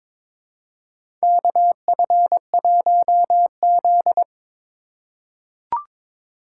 Morse code, heard as a keyed tone, about 21 wpm, 700 Hz. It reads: KF1Z
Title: USA Call Signs - 4x Character Spacing - 22wpm